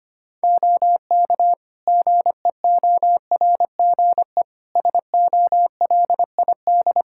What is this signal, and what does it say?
Morse code, heard as a keyed tone, about 25 wpm, 700 Hz. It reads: OK GEORGE SOLID
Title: K1USN SST QSO Practice 25wpm